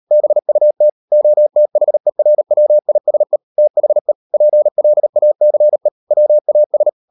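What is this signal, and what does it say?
Morse code, heard as a keyed tone, about 38 wpm, 600 Hz. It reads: BUT OTHERWISE THE PLACE WAS
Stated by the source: Morse code practice at 38 WPM, sending Ken Follett's novel Never in English, from CW med Gnister.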